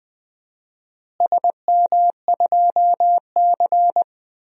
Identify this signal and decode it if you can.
Morse code, heard as a keyed tone, about 20 wpm, 700 Hz. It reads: SM2C
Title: DXCC-100 Call Signs - 4 Characters - Repeated 2x - 20wpm